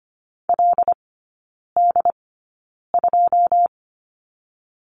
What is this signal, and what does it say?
Morse code, heard as a keyed tone, about 25 wpm, spaced as for 9 wpm, 690 Hz. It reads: LD2
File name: Lesson 14e - Number 2 - Three Characters 25wpm